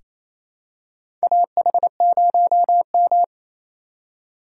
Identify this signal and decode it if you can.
Morse code, heard as a keyed tone, about 28 wpm, 700 Hz. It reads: AH0M
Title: USA Call Signs - 4 Characters - 4x Character Spacing - 28wpm